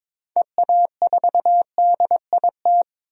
Morse code, heard as a keyed tone, about 22 wpm, 700 Hz. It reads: EA4DIT
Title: DXCC-100 Call Signs - 6 Characters - 22wpm